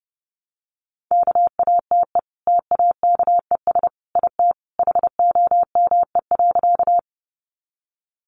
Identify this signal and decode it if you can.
Morse code, heard as a keyed tone, about 30 wpm, 700 Hz. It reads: KATE TAKES IT HOME.
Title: Easy Phrases 30wpm